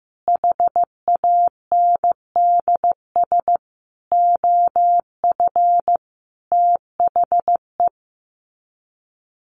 Morse code, heard as a keyed tone, about 15 wpm, 700 Hz. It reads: HANDS OF THE
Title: Sets of 5 Words 15wpm